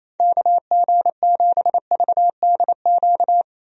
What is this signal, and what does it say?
Morse code, heard as a keyed tone, about 28 wpm, 700 Hz. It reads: KG7VDQ
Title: USA Call Signs - 6 Characters - Repeated 2x - 28wpm